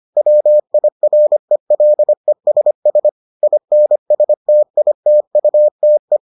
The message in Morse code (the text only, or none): WIRELESS INSTITUTE